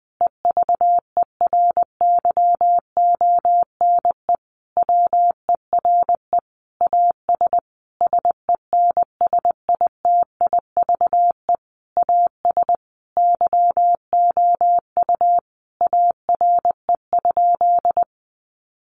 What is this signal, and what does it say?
Morse code, heard as a keyed tone, about 20 wpm, 700 Hz. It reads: EVERYONE WERE AS SENSITIVE AS YOU ARE?
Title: Sentences from Top 1400 Words 20wpm